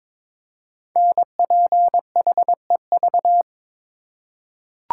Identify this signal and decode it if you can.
Morse code, heard as a keyed tone, about 23 wpm, 700 Hz. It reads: NPHEV
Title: Code Groups - 5 Letters - 22wpm